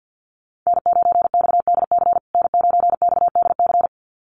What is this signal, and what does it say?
Morse code, heard as a keyed tone, about 50 wpm, 700 Hz. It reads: N9XDC N9XDC
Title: USA Call Signs Repeated 2x 50wpm